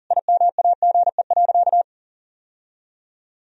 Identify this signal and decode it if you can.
Morse code, heard as a keyed tone, about 40 wpm, 700 Hz. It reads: IMAGE.
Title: Sentences from Top 1000 Words - Review Part 4 - 40wpm